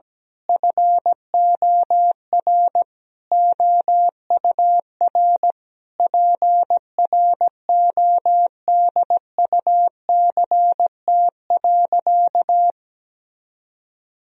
Morse code, heard as a keyed tone, about 17 wpm, 700 Hz. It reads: FOR OUR PRODUCT.
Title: Sentences from Top 1000 Words - Review Part 3 - 17wpm